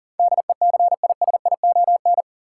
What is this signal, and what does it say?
Morse code, heard as a keyed tone, about 37 wpm, 700 Hz. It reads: DECISION
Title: Sentences from Top 1000 Words - New Word Drill - 40wpm